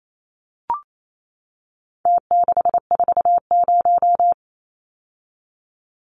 Morse code, heard as a keyed tone, about 28 wpm, 700 Hz. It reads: T640
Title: Numbers - Contest Style - 4-Digit 28wpm